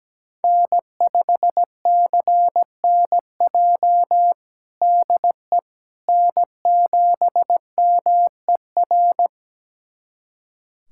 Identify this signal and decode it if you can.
Morse code, heard as a keyed tone, about 17 wpm, 700 Hz. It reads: N5CNJ DE N7MER
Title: USA Callsign DE Callsign - 5 Characters - 17wpm